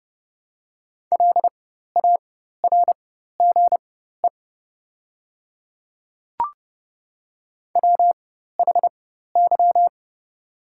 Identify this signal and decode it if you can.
Morse code, heard as a keyed tone, about 30 wpm, spaced as for 14 wpm, 700 Hz. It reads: LARGE WHY
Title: 4x Character Spacing - Top 300 Words - 30wpm